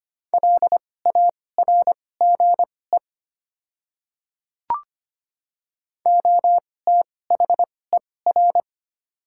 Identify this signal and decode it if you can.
Morse code, heard as a keyed tone, about 25 wpm, spaced as for 18 wpm, 700 Hz. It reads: LARGE OTHER
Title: 2x Character Spacing - Top 300 Words - 25wpm